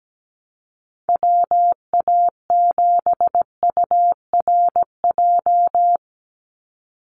Morse code, heard as a keyed tone, about 17 wpm, 700 Hz. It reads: WA7URJ
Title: USA Call Signs 17wpm